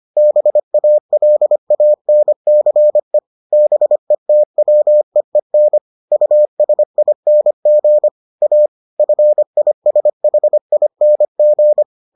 Morse code, heard as a keyed tone, about 25 wpm, 600 Hz. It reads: BALANCE BETWEEN USING A FISHING